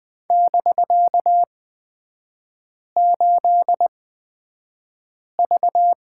Morse code, heard as a keyed tone, about 20 wpm, spaced as for 5 wpm, 700 Hz. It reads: <BK>8V
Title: 250 Lesson 40e - BK - Three Characters 20wpm